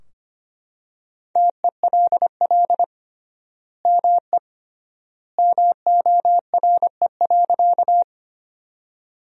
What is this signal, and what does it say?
Morse code, heard as a keyed tone, about 25 wpm, 700 Hz. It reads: TELL ME MORE.
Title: 3x Word-Spacing - Sentences from Top 300 Words - 25wpm